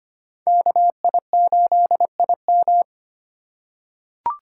Morse code, heard as a keyed tone, about 25 wpm, 700 Hz. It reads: KI8IM
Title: USA Call Signs - 5 Characters - 4x Character Spacing - 25wpm